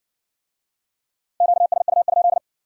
Morse code, heard as a keyed tone, about 55 wpm, 700 Hz. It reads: QSU?